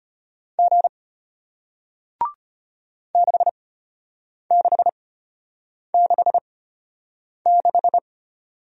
Morse code, heard as a keyed tone, about 33 wpm, spaced as for 8 wpm, 700 Hz. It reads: G 6666